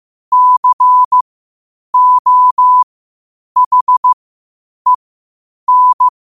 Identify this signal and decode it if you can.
Morse code, heard as a keyed tone, about 15 wpm, spaced as for 8 wpm, 1000 Hz. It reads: COHEN
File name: qotdfiles / 8wpm